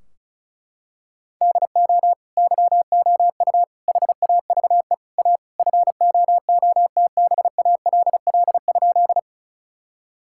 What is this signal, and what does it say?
Morse code, heard as a keyed tone, about 36 wpm, 700 Hz. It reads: DO YOU HAVE A FOOTBALL?
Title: Sentences from Top 1600 Words 35wpm